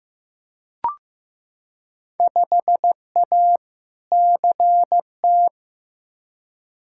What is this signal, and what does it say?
Morse code, heard as a keyed tone, about 15 wpm, 700 Hz. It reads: 5A CT
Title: Field Day Exchanges - 15wpm